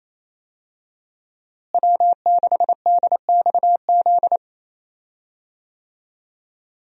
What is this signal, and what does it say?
Morse code, heard as a keyed tone, about 28 wpm, 700 Hz. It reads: W6DXZ
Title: USA Call Signs - 5 Characters - Repeated 2x - 28wpm